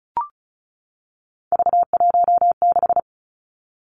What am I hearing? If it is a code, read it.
Morse code, heard as a keyed tone, about 35 wpm, 700 Hz. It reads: V16
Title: Three Challenging Letter-Number Combinations 35wpm